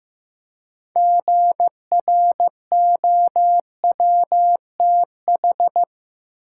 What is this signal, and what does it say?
Morse code, heard as a keyed tone, about 15 wpm, 700 Hz. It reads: GROWTH
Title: Six Letter Words - 15wpm